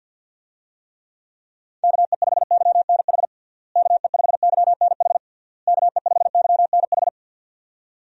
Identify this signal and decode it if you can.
Morse code, heard as a keyed tone, about 52 wpm, 700 Hz. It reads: KE5YNH KE5YNH KE5YNH
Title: USA Call Signs - 6 Characters - Repeated 3x - 50wpm